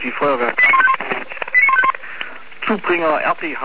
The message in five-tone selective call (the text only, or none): 89151 89151